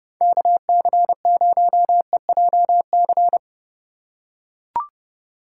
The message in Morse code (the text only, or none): KC0EJC